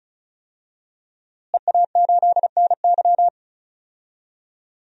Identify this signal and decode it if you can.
Morse code, heard as a keyed tone, about 34 wpm, 700 Hz. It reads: EA8NY
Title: DXCC-100 Call Signs - 5 Characters - Repeated 2x - 35wpm